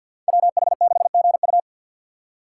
Morse code, heard as a keyed tone, about 51 wpm, 690 Hz. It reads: WH6GU
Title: USA Call Signs - 5 Characters - 6x Character Spacing - 50wpm